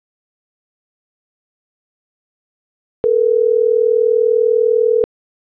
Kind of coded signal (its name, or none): telephone call progress tone